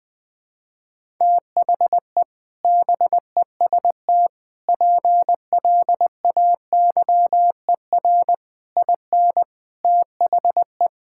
Morse code, heard as a keyed tone, about 20 wpm, 700 Hz. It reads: THE BEST PLAYER IN THE
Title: Sets of 6 Words - 20wpm